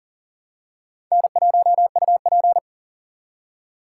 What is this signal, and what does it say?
Morse code, heard as a keyed tone, about 40 wpm, 700 Hz. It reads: N1UP
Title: USA Call Signs - 4 Characters - 3x Character Spacing - 40wpm